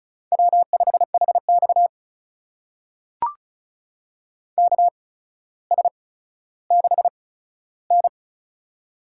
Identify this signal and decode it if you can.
Morse code, heard as a keyed tone, about 36 wpm, spaced as for 15 wpm, 700 Hz. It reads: W5HX KS6N
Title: USA Call Signs - 8x Character Spacing - 35wpm